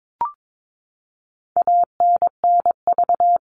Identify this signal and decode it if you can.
Morse code, heard as a keyed tone, about 22 wpm, 700 Hz. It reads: ANNV
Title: All Cut Numbers - 4-Digit 22wpm